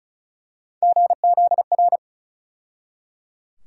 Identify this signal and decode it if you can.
Morse code, heard as a keyed tone, about 34 wpm, 700 Hz. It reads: GZR
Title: USA Call Sign Postfixes - 3 Letters 35wpm